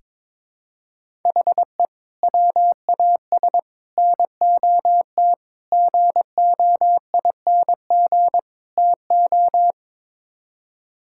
Morse code, heard as a keyed tone, about 22 wpm, 700 Hz. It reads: HE WAS NOT GOING TO
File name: practice / 22wpm / Build A Phrase - 5-Word Phrases - 22wpm